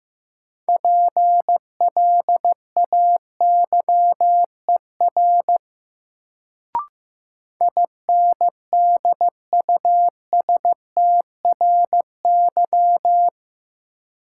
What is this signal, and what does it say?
Morse code, heard as a keyed tone, about 15 wpm, 700 Hz. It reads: PLAYER INDUSTRY
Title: Sentences from Top 1000 Words - Review Part 4 - New Word Drill - 15wpm